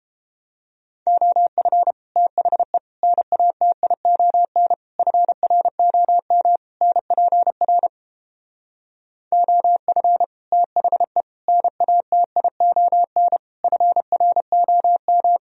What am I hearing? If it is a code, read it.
Morse code, heard as a keyed tone, about 31 wpm, 700 Hz. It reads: OF THE NATION FROM NPR OF THE NATION FROM